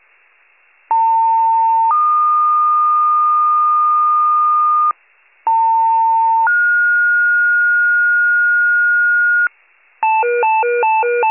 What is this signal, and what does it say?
Two-tone page, heard as a one-second tone, then a three-second tone, about 900 then 1220 Hz, about 900 then 1430 Hz, with noise. It is heard from 0.9 to 9.5 s.
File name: wcalert